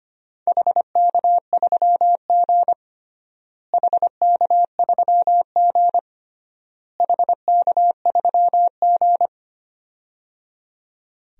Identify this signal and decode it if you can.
Morse code, heard as a keyed tone, about 25 wpm, 700 Hz. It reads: HK3G HK3G HK3G
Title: DXCC-100 Call Signs - 4 Characters - Repeated 3x - 25wpm